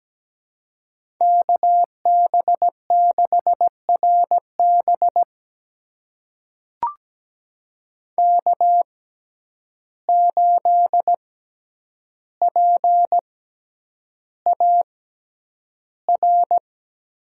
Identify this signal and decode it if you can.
Morse code, heard as a keyed tone, about 17 wpm, spaced as for 9 wpm, 700 Hz. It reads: KB6RB K8PAR
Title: USA Call Signs - 5 Characters - 6x Character Spacing - 17wpm